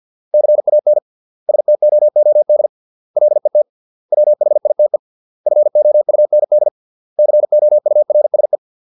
Morse code, heard as a keyed tone, about 50 wpm, 600 Hz. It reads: CAN STOOD LET WHITE FOUND COURSE